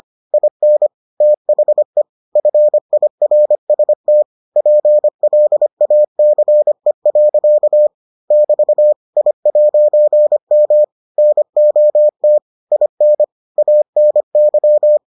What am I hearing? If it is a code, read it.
Morse code, heard as a keyed tone, about 25 wpm, 600 Hz. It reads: IN THE FIRST PLACE. <BT> I'M NOT IN ANY